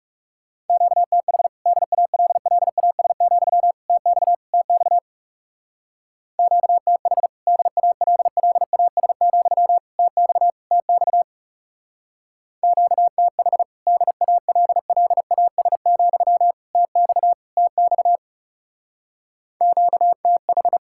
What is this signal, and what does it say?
Morse code, heard as a keyed tone, about 36 wpm, 700 Hz. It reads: QTH DALLAS, TX TX QTH DALLAS, TX TX QTH DALLAS, TX TX QTH
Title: Speed-Racing Common QSO Phrases - 30wpm